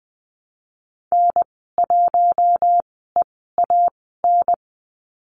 Morse code, heard as a keyed tone, about 20 wpm, spaced as for 14 wpm, 700 Hz. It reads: N1EAN
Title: 042 Lesson 8g - Number 1 - Call Signs 20wpm